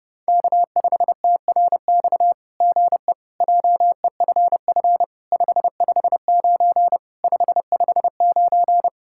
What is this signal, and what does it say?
Morse code, heard as a keyed tone, about 31 wpm, 700 Hz. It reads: K5TRX GE JEFF 559 559
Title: SOTA QSO Practice 30wpm